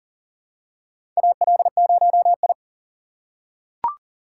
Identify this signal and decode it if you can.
Morse code, heard as a keyed tone, about 40 wpm, 700 Hz. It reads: AL0I
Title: USA Call Signs - 4x Character Spacing - 40wpm